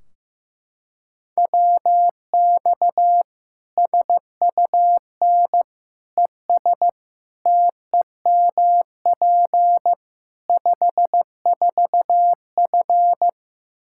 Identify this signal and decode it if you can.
Morse code, heard as a keyed tone, about 15 wpm, 700 Hz. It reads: WX SUN ES TEMP 54F
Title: Practice QSO - Exchange 2 - Info - 15wpm